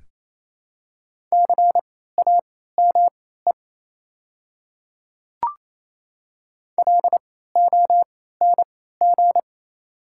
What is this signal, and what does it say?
Morse code, heard as a keyed tone, about 27 wpm, spaced as for 16 wpm, 700 Hz. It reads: CAME LONG
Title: 3x Character Spacing - Top 300 Words - 28wpm